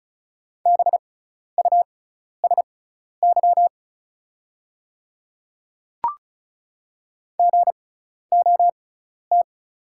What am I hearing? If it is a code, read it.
Morse code, heard as a keyed tone, about 36 wpm, spaced as for 12 wpm, 700 Hz. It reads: BUSY GOT
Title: 6x Character Spacing - Top 500 Words - 35wpm